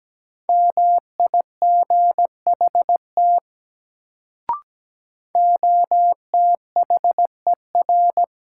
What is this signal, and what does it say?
Morse code, heard as a keyed tone, about 17 wpm, 700 Hz. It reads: MIGHT OTHER
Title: Sets of 2 Words - New Word Drill - 17wpm